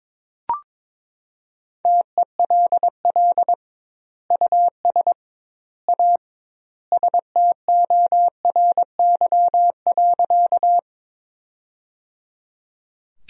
Morse code, heard as a keyed tone, about 22 wpm, 700 Hz. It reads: TELL US A STORY.
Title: 2x Word-Spacing - Sentences from Top 300 Words - 22wpm